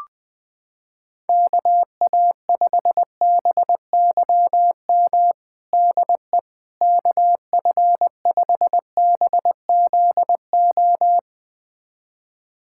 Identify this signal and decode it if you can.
Morse code, heard as a keyed tone, about 21 wpm, 700 Hz. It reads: KA5BYM DE KF5BZO
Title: USA Callsign DE Callsign 20wpm